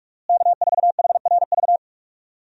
Morse code, heard as a keyed tone, about 42 wpm, 700 Hz. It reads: K4HRV
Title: USA Call Signs - 5 Characters - Repeated 2x - 45wpm